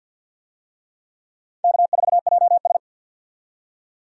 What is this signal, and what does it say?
Morse code, heard as a keyed tone, about 46 wpm, 700 Hz. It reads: K4JS
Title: USA Call Signs - 4 Characters - 50wpm